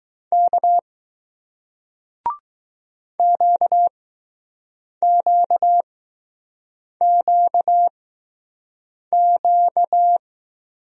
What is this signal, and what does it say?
Morse code, heard as a keyed tone, about 20 wpm, spaced as for 6 wpm, 700 Hz. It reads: K QQQQ